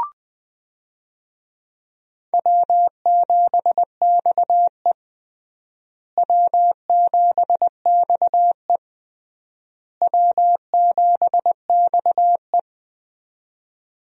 Morse code, heard as a keyed tone, about 21 wpm, 700 Hz. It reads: W7XE W7XE W7XE
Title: USA Call Signs - 4 Characters - Repeated 3x - 20wpm